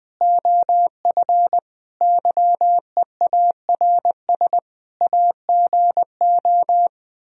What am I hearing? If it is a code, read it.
Morse code, heard as a keyed tone, about 20 wpm, 700 Hz. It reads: OF YEARS AGO
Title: Speed-Racing - Sets of 5 Words 17wpm